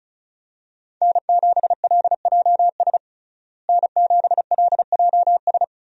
Morse code, heard as a keyed tone, about 34 wpm, 700 Hz. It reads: N7LJS N7LJS
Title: USA Call Signs - 5 Characters - Repeated 2x - 35wpm